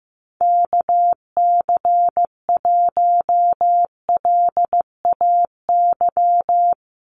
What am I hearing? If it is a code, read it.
Morse code, heard as a keyed tone, about 15 wpm, 700 Hz. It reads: KC1LAY
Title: USA Call Signs Repeated 2x 15wpm